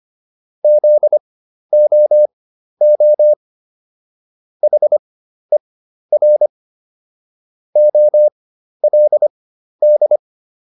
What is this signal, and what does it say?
Morse code, heard as a keyed tone, about 25 wpm, spaced as for 12 wpm, 600 Hz. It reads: ZOO HER OLD